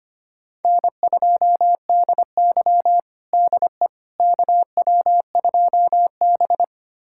Morse code, heard as a keyed tone, about 25 wpm, 700 Hz. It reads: N2DY DE KW2B
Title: USA Callsign DE Callsign - 4 Characters - 25wpm